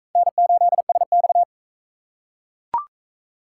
Morse code, heard as a keyed tone, about 37 wpm, 700 Hz. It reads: N8SX